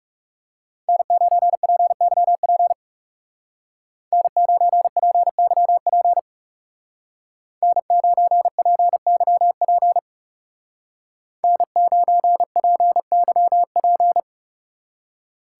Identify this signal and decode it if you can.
Morse code, heard as a keyed tone, about 35 wpm, 700 Hz. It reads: N9PYP N9PYP N9PYP N9PYP
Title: Speed Racing - USA 1x3 Call Signs 30wpm